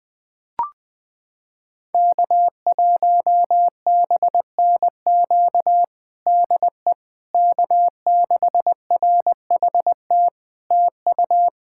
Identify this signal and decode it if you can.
Morse code, heard as a keyed tone, about 20 wpm, 700 Hz. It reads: K1BNQ DE K6RHT TU
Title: K1USN SST QSO Practice 20wpm